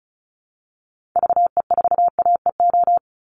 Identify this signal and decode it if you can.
Morse code, heard as a keyed tone, about 34 wpm, 700 Hz. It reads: VE4AEO
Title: Call Signs Repeated 3x - 35wpm